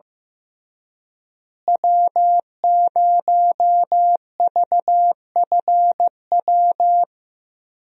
Morse code, heard as a keyed tone, about 15 wpm, 700 Hz. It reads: W0VFW
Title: USA Call Signs - 5 Characters - Repeated 2x - 15wpm